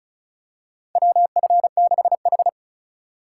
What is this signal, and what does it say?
Morse code, heard as a keyed tone, about 36 wpm, 700 Hz. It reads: WF6H
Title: USA Call Signs - 4 Characters - 3x Character Spacing - 35wpm